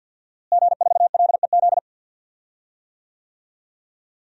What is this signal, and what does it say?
Morse code, heard as a keyed tone, about 52 wpm, 700 Hz. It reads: G4LEZ
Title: DXCC-100 Call Signs - 5 Characters - Repeated 2x - 50wpm